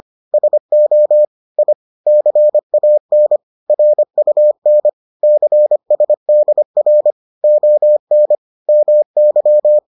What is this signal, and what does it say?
Morse code, heard as a keyed tone, about 25 wpm, 600 Hz. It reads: SO I CAN RUN CSDR ON MY